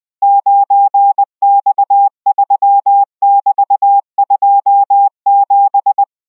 Morse code, heard as a keyed tone, about 20 wpm, 800 Hz. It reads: 9X3<BT>27